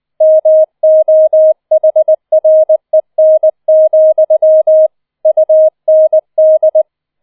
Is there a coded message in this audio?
Yes: Morse code from 0.2 to 6.8 s, heard as a keyed tone, about 20 wpm, 630 Hz. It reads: MOHREN, UND